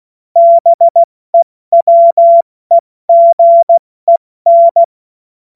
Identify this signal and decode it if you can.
Morse code, heard as a keyed tone, about 16 wpm, 680 Hz. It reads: BEWEGEN
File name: Morsekurs / Witze / Witz 20 Februar 2024 70 F80